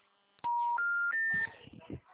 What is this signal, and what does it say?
Special information tone, with speech.
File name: out- -2034- - - .83